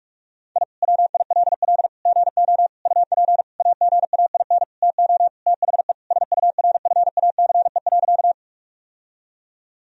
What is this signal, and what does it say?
Morse code, heard as a keyed tone, about 41 wpm, 700 Hz. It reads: I WILL GO UP AGAIN TO THE SURFACE.
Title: Sentences from Top 1000 Words - Review Part 3 - 45wpm